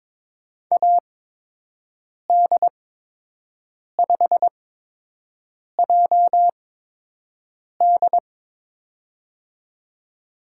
Morse code, heard as a keyed tone, about 21 wpm, spaced as for 6 wpm, 700 Hz. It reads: AD5JD
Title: USA Call Signs - 5 Characters - 8x Character Spacing - 22wpm